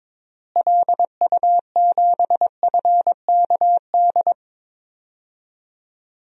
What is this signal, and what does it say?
Morse code, heard as a keyed tone, about 21 wpm, 700 Hz. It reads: LU7FKD
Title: DXCC-100 Call Signs - 6 Characters - Repeated 2x - 22wpm